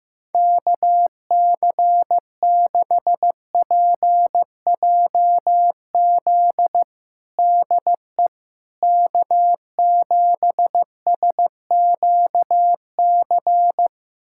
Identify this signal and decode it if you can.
Morse code, heard as a keyed tone, about 15 wpm, 700 Hz. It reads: KC6PJZ DE K7SQC
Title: USA Callsign DE Callsign 15wpm